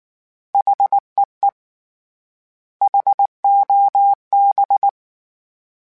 Morse code, heard as a keyed tone, about 19 wpm, 800 Hz. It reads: HEE HOB
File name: H Prefixes / H mp3s